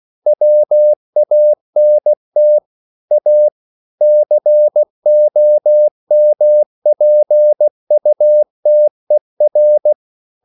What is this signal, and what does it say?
Morse code, heard as a keyed tone, about 16 wpm, 600 Hz. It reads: WANT A COMPUTER
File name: CW / Practice / EnglishQuotations0416